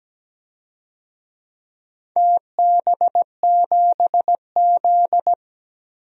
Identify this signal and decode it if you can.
Morse code, heard as a keyed tone, about 17 wpm, 700 Hz. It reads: TB7Z
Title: DXCC-100 Call Signs - 4 Characters - 17wpm